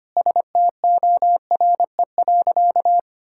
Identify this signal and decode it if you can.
Morse code, heard as a keyed tone, about 25 wpm, 700 Hz. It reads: STORE.
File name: practice / 25wpm / Sentences with 4-Word Phrases - 25wpm